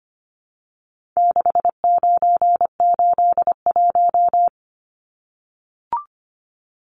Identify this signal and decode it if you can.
Morse code, heard as a keyed tone, about 25 wpm, 700 Hz. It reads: 6981
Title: Numbers - 4-Digit 25wpm